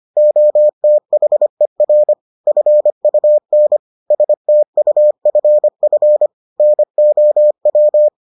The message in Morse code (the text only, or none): OTHER FUN STUFF NOW